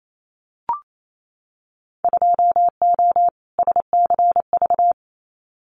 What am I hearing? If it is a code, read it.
Morse code, heard as a keyed tone, about 28 wpm, 700 Hz. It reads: 2O SCV
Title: Winter Field Day Exchanges Encore 28wpm